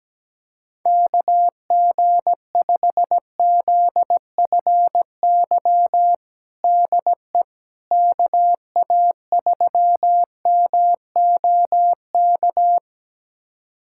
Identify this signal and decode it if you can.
Morse code, heard as a keyed tone, about 17 wpm, 700 Hz. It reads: KG5ZFY DE KA3MOK
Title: USA Callsign DE Callsign - 6 Characters - 17wpm